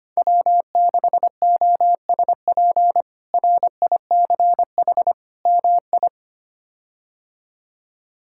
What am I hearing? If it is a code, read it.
Morse code, heard as a keyed tone, about 25 wpm, 700 Hz. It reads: W6OSP RICH MI
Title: K1USN SST QSO Practice - 25wpm